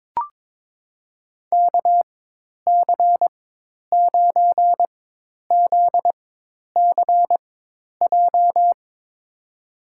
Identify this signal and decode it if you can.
Morse code, heard as a keyed tone, about 22 wpm, spaced as for 10 wpm, 700 Hz. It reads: KC9ZCJ
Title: USA Call Signs - 6 Characters - 4x Character Spacing - 22wpm